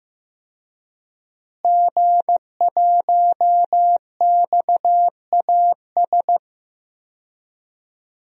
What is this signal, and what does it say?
Morse code, heard as a keyed tone, about 15 wpm, 700 Hz. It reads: G1XAS
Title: DXCC-100 Call Signs - 5 Characters - Repeated 3x - 15wpm